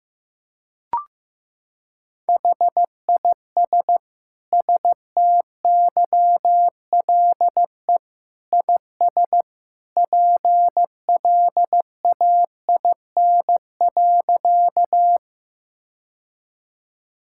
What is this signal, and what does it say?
Morse code, heard as a keyed tone, about 16 wpm, 700 Hz. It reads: HIS STYLE IS PLAIN.
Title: Sentences from Top 1000 Words - Review Part 5 - 15wpm